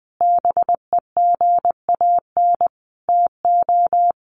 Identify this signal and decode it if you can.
Morse code, heard as a keyed tone, about 20 wpm, 700 Hz. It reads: BEGAN TO
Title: Sentences from Top 1100 Words 20wpm